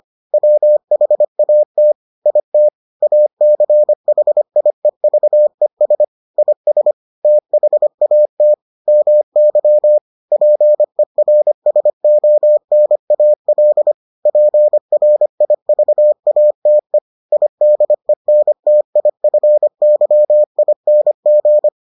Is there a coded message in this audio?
Yes: Morse code from 0.3 to 21.7 s, heard as a keyed tone, about 25 wpm, 600 Hz. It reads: WHAT IT ACHIEVES IS THAT MY PERSONAL PRIVATE IDENTIFYING